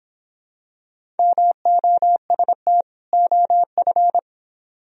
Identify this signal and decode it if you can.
Morse code, heard as a keyed tone, about 26 wpm, 700 Hz. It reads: MOST OF
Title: Speed-Racing - Sets of 2 Words 22wpm